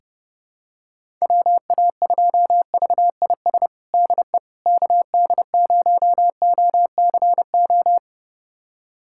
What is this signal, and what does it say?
Morse code, heard as a keyed tone, about 30 wpm, 700 Hz. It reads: WA2VIS DE KD0OCO
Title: USA Callsign DE Callsign - 6 Characters - 30wpm